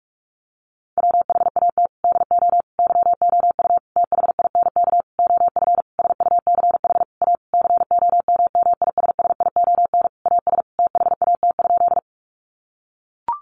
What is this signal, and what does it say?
Morse code, heard as a keyed tone, about 44 wpm, 700 Hz. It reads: WHAT DO YOU THINK OF SUCH A COMMISSION AS THAT?
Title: Sentences from Top 1400 Words 45wpm